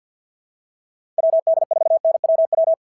Morse code, heard as a keyed tone, about 51 wpm, 690 Hz. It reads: WD4NWW
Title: SOTA QSO Practice 50wpm